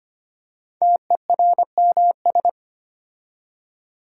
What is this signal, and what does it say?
Morse code, heard as a keyed tone, about 25 wpm, 700 Hz. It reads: TERMS
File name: 5 Letter Words - First 3 Letters Given - 25wpm